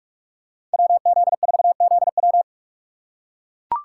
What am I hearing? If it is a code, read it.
Morse code, heard as a keyed tone, about 42 wpm, 700 Hz. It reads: WZ4ZW
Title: USA Call Signs - 5 Characters - 6x Character Spacing - 45wpm